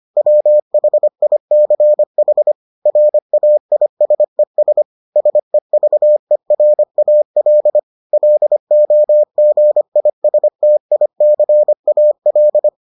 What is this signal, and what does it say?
Morse code, heard as a keyed tone, about 25 wpm, 600 Hz. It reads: WHICH RAISES SEVERAL LOGISTICAL